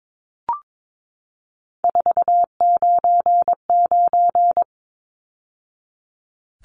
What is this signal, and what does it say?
Morse code, heard as a keyed tone, about 22 wpm, 700 Hz. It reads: 499
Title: Numbers - 3-Digit 22wpm